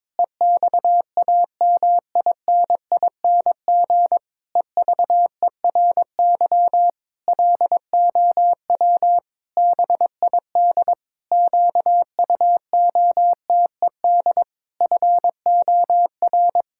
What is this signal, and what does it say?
Morse code, heard as a keyed tone, about 22 wpm, 700 Hz. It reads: EXAMINING EVERY LOW BID QUOTED FOR
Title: Pangram Challenge v2 - 22wpm